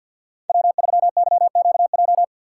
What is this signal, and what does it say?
Morse code, heard as a keyed tone, about 53 wpm, 700 Hz. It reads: W3YQJ
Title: Code Groups - 5 Letter:Numbers - 50wpm